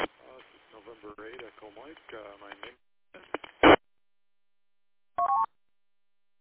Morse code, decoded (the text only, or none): none